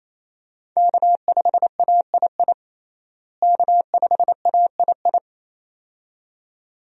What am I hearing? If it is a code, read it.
Morse code, heard as a keyed tone, about 27 wpm, 700 Hz. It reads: K5AII K5AII
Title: USA Call Signs - 5 Characters - Repeated 2x - 28wpm